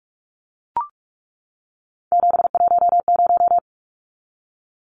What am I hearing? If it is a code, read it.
Morse code, heard as a keyed tone, about 44 wpm, 700 Hz. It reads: 710
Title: Numbers - 3-Digit 45wpm